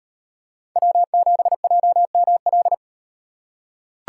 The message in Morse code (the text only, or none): W7JML